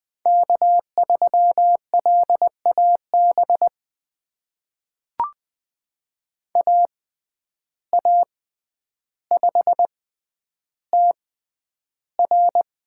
Morse code, heard as a keyed tone, about 20 wpm, spaced as for 10 wpm, 700 Hz. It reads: K3LAB AA5TR
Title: USA Call Signs - 6x Character Spacing - 20wpm